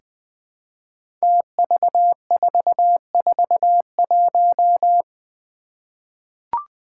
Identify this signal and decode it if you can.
Morse code, heard as a keyed tone, about 20 wpm, 700 Hz. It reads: TV441
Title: Five Challenging Letter-Number Combinations 20wpm